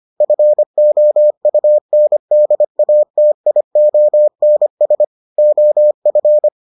Morse code, heard as a keyed tone, about 25 wpm, 600 Hz. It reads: FOUNDATIONS OF